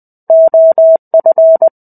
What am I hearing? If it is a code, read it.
Morse code, heard as a keyed tone, about 20 wpm, 650 Hz. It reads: OF
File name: 05-Chapter-05 20 WPM